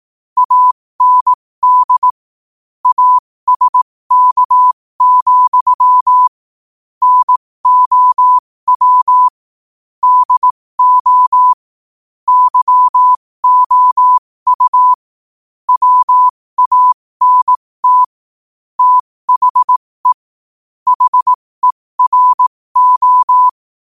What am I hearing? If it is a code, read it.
Morse code, heard as a keyed tone, about 18 wpm, spaced as for 15 wpm, 1000 Hz. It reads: AND ASK, NOW DO YOU WANT THE HERO
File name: qotdfiles / 15wpm